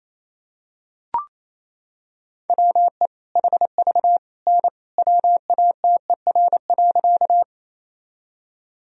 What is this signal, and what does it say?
Morse code, heard as a keyed tone, about 28 wpm, 700 Hz. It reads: WE HV N WATER.
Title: Sentences from Top 100 Words and Abbreviations 28wpm